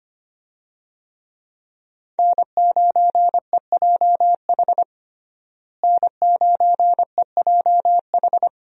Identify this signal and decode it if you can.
Morse code, heard as a keyed tone, about 25 wpm, 700 Hz. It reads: N9EJH N9EJH
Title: USA Call Signs - 5 Characters - Repeated 3x - 25wpm